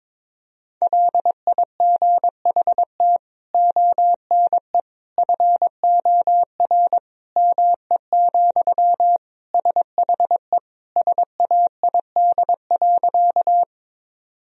Morse code, heard as a keyed tone, about 22 wpm, 700 Hz. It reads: LIGHT ONE FOR ME, SHE SAID.
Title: Sentences from Top 1000 Words - Review Part 1 - 22wpm